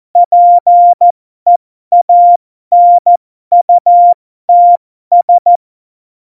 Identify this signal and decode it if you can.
Morse code, heard as a keyed tone, about 14 wpm, spaced as for 12 wpm, 700 Hz. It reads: PEANUTS
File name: Morsekurs / Witze